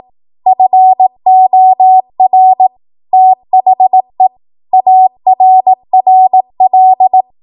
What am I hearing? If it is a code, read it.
Morse code, heard as a keyed tone, about 18 wpm, 750 Hz. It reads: FOR THE ARRL
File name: Archive / 18 WPM / 18WPM